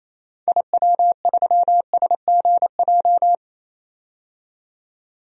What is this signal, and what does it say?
Morse code, heard as a keyed tone, about 28 wpm, 700 Hz. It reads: IW3SGJ
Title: DXCC-100 Call Signs - 6 Characters - 28wpm